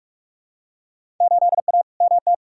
Morse code, heard as a keyed tone, about 47 wpm, 700 Hz. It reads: 8A MT
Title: Field Day Exchanges Encore - 45wpm